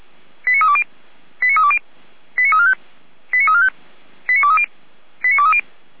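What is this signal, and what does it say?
Five-tone selective call (ZVEI), heard as quick runs of five tones, tones of about 70 ms. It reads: 89320 89320 89345 89345 89230 89230
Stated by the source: ZVEI five-tone selective call received on SDR.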